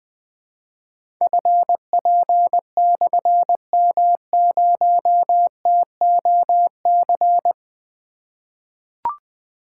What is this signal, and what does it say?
Morse code, heard as a keyed tone, about 20 wpm, 700 Hz. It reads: FP/M0TOC
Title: Difficult Call Signs 20wpm